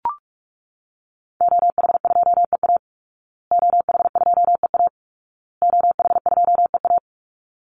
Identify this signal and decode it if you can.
Morse code, heard as a keyed tone, about 44 wpm, 700 Hz. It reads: OH2EA OH2EA OH2EA
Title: Call Signs Repeated 2x - 45wpm